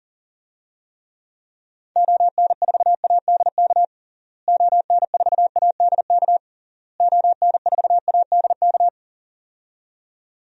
Morse code, heard as a keyed tone, about 40 wpm, 700 Hz. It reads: ON4ADK ON4ADK ON4ADK
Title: DXCC-100 Call Signs - 6 Characters - Repeated 3x - 40wpm